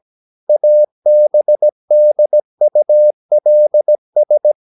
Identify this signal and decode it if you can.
Morse code, heard as a keyed tone, about 17 wpm, 600 Hz. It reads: ABDULS
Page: Romanen Never af Ken Follet på engelsk.